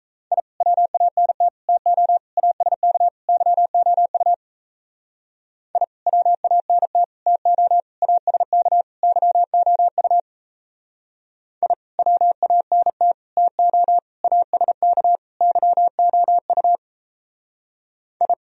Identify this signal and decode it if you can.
Morse code, heard as a keyed tone, about 34 wpm, 700 Hz. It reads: I WANT TO ASK YOU I WANT TO ASK YOU I WANT TO ASK YOU I
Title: Speed-Racing - Sets of 5 Words 28wpm